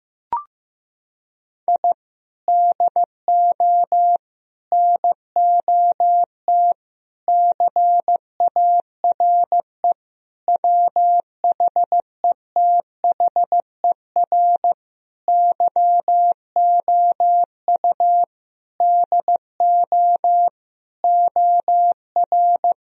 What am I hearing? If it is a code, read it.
Morse code, heard as a keyed tone, about 15 wpm, 700 Hz. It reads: I DO NOT CARE WHETHER YOU DO OR
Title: Sentences from Top 1000 Words - Review Part 3 - 15wpm